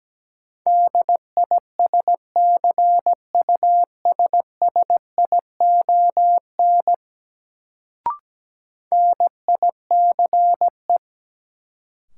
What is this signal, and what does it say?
Morse code, heard as a keyed tone, about 17 wpm, 700 Hz. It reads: DISCUSSION NICE
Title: Sentences from Top 1100 Words - New Word Drill Alt - 17wpm